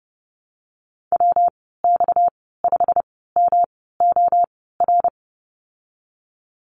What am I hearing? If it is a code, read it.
Morse code, heard as a keyed tone, about 32 wpm, spaced as for 17 wpm, 700 Hz. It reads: WX5MOR
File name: Lesson 37g - Letter x - Call Signs 30wpm